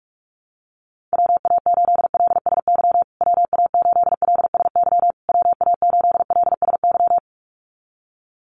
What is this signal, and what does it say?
Morse code, heard as a keyed tone, about 44 wpm, 700 Hz. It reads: WA8LSY WA8LSY WA8LSY
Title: USA Call Signs Repeated 3x 45wpm